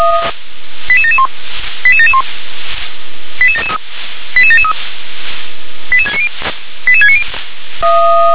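Five-tone selective call (ZVEI), heard as quick runs of five tones, tones of about 70 ms. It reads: 88771 88771 88773 88773 88690 88690
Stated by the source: ZVEI five-tone selective call received on an SDR.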